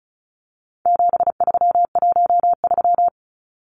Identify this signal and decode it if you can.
Morse code, heard as a keyed tone, about 34 wpm, 700 Hz. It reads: 7313
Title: Numbers - 4-Digit 35wpm